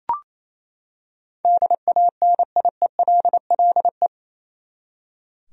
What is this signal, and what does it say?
Morse code, heard as a keyed tone, about 28 wpm, 700 Hz. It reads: DANIELLE
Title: USA Common Names - 28wpm